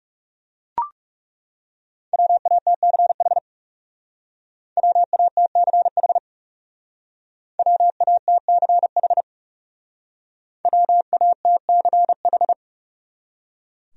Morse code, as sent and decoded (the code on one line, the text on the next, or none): .-- .- - -.-. .... / .-- .- - -.-. .... / .-- .- - -.-. .... / .-- .- - -.-. ....
WATCH WATCH WATCH WATCH